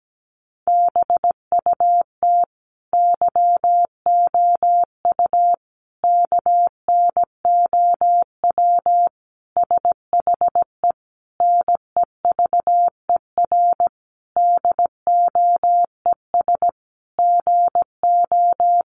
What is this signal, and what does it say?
Morse code, heard as a keyed tone, about 17 wpm, 700 Hz. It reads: BUT YOU KNOW SHE NEVER DOES GO
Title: Sentences from Top 1400 Words 17wpm